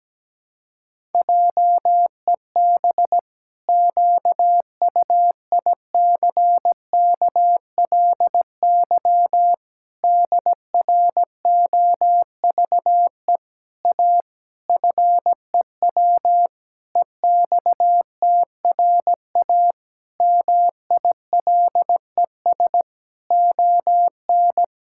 Morse code, heard as a keyed tone, about 17 wpm, 700 Hz. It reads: JEB QUICKLY DROVE A FEW EXTRA MILES ON
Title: Pangram Challenge v2 - 17wpm